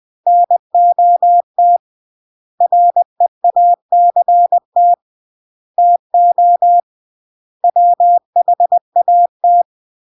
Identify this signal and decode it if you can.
Morse code, heard as a keyed tone, about 20 wpm, 700 Hz. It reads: NOT REACT TO WHAT
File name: CW / jodorowsky / 82 commandments 20-20